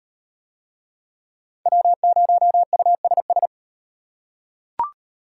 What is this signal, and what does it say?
Morse code, heard as a keyed tone, about 34 wpm, 700 Hz. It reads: W0USS